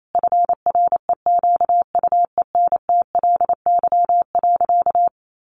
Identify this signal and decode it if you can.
Morse code, heard as a keyed tone, about 28 wpm, 700 Hz. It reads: FREQUENTLY.
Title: Sentences from Top 1300 Words 28wpm